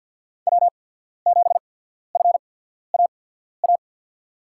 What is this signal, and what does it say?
Morse code, heard as a keyed tone, about 53 wpm, spaced as for 14 wpm, 700 Hz. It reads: W7FAA
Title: USA Call Signs - 5 Characters - 8x Character Spacing - 50wpm